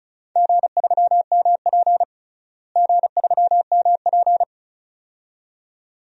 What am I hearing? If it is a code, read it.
Morse code, heard as a keyed tone, about 34 wpm, 700 Hz. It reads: G3MP G3MP
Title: DXCC-100 Call Signs - 4 Characters - Repeated 2x - 35wpm